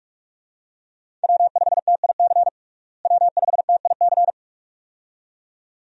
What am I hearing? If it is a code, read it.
Morse code, heard as a keyed tone, about 46 wpm, 700 Hz. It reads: W5TIC W5TIC
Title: USA Call Signs - 5 Characters - Repeated 2x - 45wpm